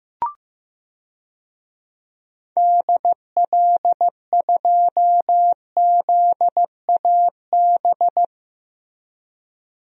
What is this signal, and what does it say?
Morse code, heard as a keyed tone, about 16 wpm, 700 Hz. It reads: DL2ZAB
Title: DXCC-100 Call Signs - 6 Characters - Repeated 3x - 15wpm